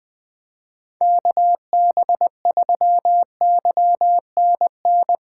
Morse code, heard as a keyed tone, about 20 wpm, 700 Hz. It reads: KB3YNN
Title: USA Callsign DE Callsign - 6 Characters - 20wpm